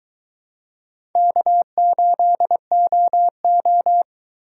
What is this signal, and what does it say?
Morse code, heard as a keyed tone, about 23 wpm, 700 Hz. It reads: K8OO